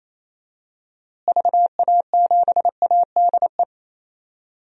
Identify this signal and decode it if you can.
Morse code, heard as a keyed tone, about 28 wpm, 700 Hz. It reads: VA7ADE
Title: DXCC-100 Call Signs - 6 Characters - Repeated 2x - 28wpm